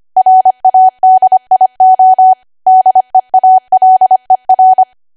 Morse code, heard as a keyed tone, about 25 wpm, 750 Hz. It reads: RADIO DEALER
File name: Archive / 25 WPM / 25WPM